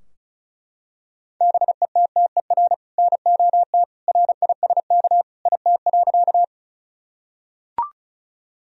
Morse code, heard as a keyed tone, about 36 wpm, 700 Hz. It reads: BETTER NOT RISK IT.
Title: Sentences from Top 1000 Words - Review Part 4 - 35wpm